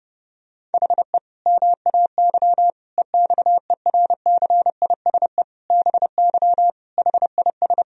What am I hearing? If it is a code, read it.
Morse code, heard as a keyed tone, about 30 wpm, 700 Hz. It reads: HE MAY EXERCISE BY HIS
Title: Sentences from Top 1000 Words - Review Part 4 - 30wpm